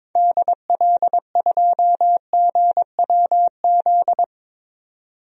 Morse code, heard as a keyed tone, about 22 wpm, 700 Hz. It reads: DL2GWZ
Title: DXCC-100 Call Signs - 6 Characters - 22wpm